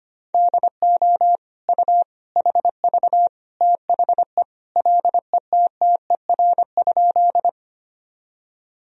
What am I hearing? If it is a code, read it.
Morse code, heard as a keyed tone, about 25 wpm, 700 Hz. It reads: DO U HV THE LETTER?
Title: Sentences from Top 300 Words and Abbreviations 25wpm